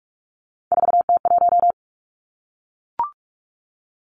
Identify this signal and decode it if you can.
Morse code, heard as a keyed tone, about 46 wpm, 700 Hz. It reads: 4T1
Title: Three Challenging Letter-Number Combinations 45wpm